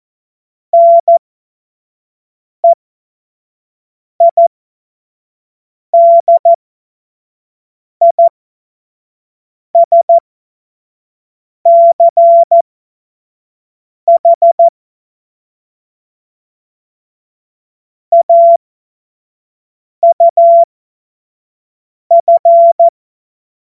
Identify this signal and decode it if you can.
Morse code, heard as a keyed tone, about 14 wpm, spaced as for 5 wpm, 680 Hz. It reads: NEIDISCH AUF
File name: Morsekurs / Witze / Witz 15. November 2022 25 F70